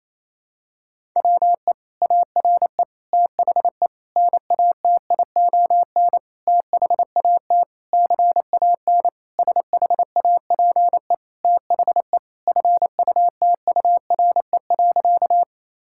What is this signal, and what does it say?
Morse code, heard as a keyed tone, about 28 wpm, 700 Hz. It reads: WE ARE THE NATION THAT CAN SHAPE THE FUTURE.
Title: Sentences from Top 1000 Words - Review Part 6 - 28wpm